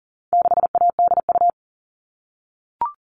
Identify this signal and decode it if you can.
Morse code, heard as a keyed tone, about 40 wpm, 700 Hz. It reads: 6ADU
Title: All Cut Numbers - 4-Digit 40wpm